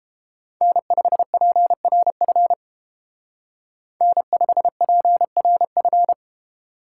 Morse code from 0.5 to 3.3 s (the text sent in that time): N5PRF